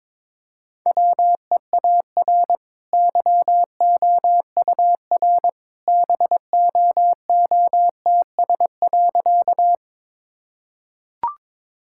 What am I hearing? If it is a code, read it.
Morse code, heard as a keyed tone, about 22 wpm, 690 Hz. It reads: WEAR YOUR BOOTS.
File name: practice / 22wpm / Easy Sentences with Hints - 22wpm